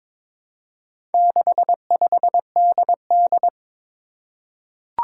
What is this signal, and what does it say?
Morse code, heard as a keyed tone, about 22 wpm, 700 Hz. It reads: 65DD
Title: Four Challenging Letter-Number Combinations 22wpm